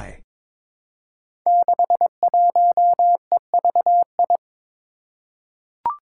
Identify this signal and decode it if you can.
Morse code, heard as a keyed tone, about 22 wpm, 700 Hz. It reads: 61EVI
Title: Five Challenging Letter-Number Combinations 22wpm